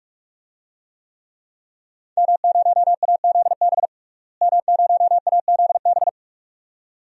Morse code, heard as a keyed tone, about 44 wpm, 700 Hz. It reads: M0AZB M0AZB
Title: DXCC-100 Call Signs - 5 Characters - Repeated 2x - 45wpm